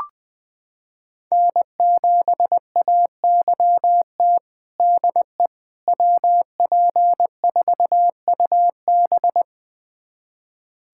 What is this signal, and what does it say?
Morse code, heard as a keyed tone, about 20 wpm, 700 Hz. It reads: N7AYT DE WP4UB
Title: USA Callsign DE Callsign - 5 Characters - 20wpm